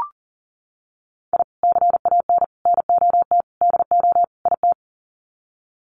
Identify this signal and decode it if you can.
Morse code, heard as a keyed tone, about 38 wpm, 700 Hz. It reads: I CAN NOT DO IT
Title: Sets of 5 Words Encore 40wpm